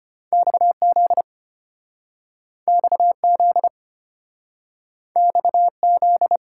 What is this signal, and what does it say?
Morse code, heard as a keyed tone, about 33 wpm, 690 Hz. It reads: XZ XZ XZ